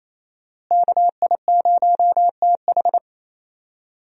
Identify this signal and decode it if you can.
Morse code, heard as a keyed tone, about 28 wpm, 700 Hz. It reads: KI0TH
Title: USA Call Signs - 5 Characters - Repeated 3x - 28wpm